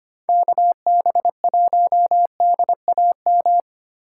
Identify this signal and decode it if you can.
Morse code, heard as a keyed tone, about 25 wpm, 700 Hz. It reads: KB1DAM
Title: USA Call Signs - 6 Characters - 3x Character Spacing - 25wpm